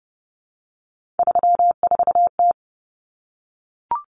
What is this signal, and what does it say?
Morse code, heard as a keyed tone, about 30 wpm, 700 Hz. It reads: 34T
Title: Common Cut Numbers - 3-Digit 30wpm